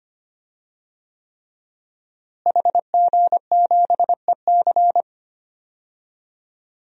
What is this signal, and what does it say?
Morse code, heard as a keyed tone, about 25 wpm, 700 Hz. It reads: HG7EC
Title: DXCC-100 Call Signs - 5 Characters - Repeated 3x - 25wpm